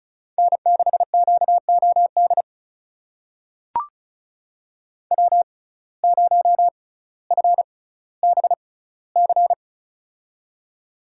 Morse code, heard as a keyed tone, about 36 wpm, spaced as for 18 wpm, 700 Hz. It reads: N6QOD W0FBC
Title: USA Call Signs - 5 Characters - 6x Character Spacing - 35wpm